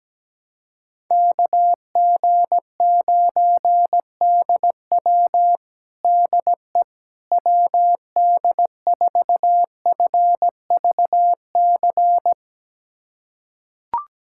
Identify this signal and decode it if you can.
Morse code, heard as a keyed tone, about 17 wpm, 700 Hz. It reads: KG9DW DE WD4FVC
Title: USA Callsign DE Callsign 17wpm